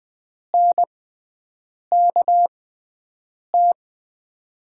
Morse code, heard as a keyed tone, about 21 wpm, spaced as for 7 wpm, 700 Hz. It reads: NKT
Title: USA Call Signs - 6x Character Spacing - 20wpm